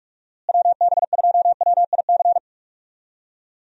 Morse code, heard as a keyed tone, about 46 wpm, 700 Hz. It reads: WB2WIC
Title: USA Call Signs - 6 Characters - Repeated 3x - 45wpm